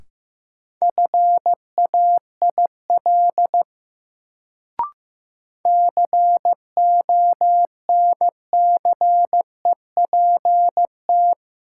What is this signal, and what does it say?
Morse code, heard as a keyed tone, about 15 wpm, 700 Hz. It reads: FAIL CONCEPT
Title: Sentences from Top 1100 Words - New Word Drill Alt - 15wpm